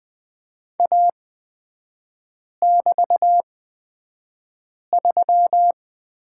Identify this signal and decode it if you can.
Morse code, heard as a keyed tone, about 20 wpm, spaced as for 5 wpm, 700 Hz. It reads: A<BT>3
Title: 224 Lesson 36e - BT - Three Characters 20wpm